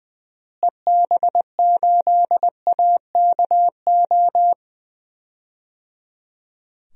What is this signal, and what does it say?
Morse code, heard as a keyed tone, about 21 wpm, 700 Hz. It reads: EB8AKO
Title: DXCC-100 Call Signs - 6 Characters - Repeated 3x - 20wpm